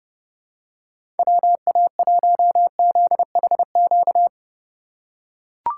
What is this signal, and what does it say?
Morse code, heard as a keyed tone, about 30 wpm, 700 Hz. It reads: WA1ZHQ
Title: USA Call Signs - 6 Characters - 2x Character Spacing - 30wpm